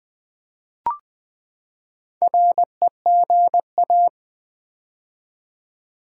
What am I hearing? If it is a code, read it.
Morse code, heard as a keyed tone, about 20 wpm, 700 Hz. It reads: REGA
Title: Build A Word - 4 Letters At A Time - 10-Letter Words - 20wpm